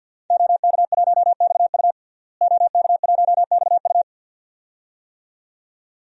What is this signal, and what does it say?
Morse code, heard as a keyed tone, about 48 wpm, 700 Hz. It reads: OK1XU OK1XU
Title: DXCC-100 Call Signs - 5 Characters - Repeated 2x - 50wpm